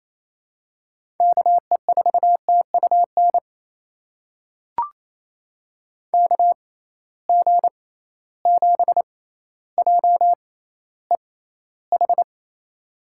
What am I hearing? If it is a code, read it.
Morse code, heard as a keyed tone, about 28 wpm, spaced as for 14 wpm, 700 Hz. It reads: KE4TUN KG7JEH
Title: USA Call Signs - 6 Characters - 6x Character Spacing - 28wpm